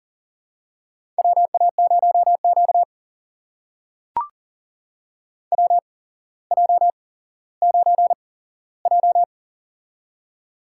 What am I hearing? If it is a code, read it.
Morse code, heard as a keyed tone, about 40 wpm, spaced as for 17 wpm, 700 Hz. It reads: WA0Q WJ9J
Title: USA Call Signs - 4 Characters - 8x Character Spacing - 40wpm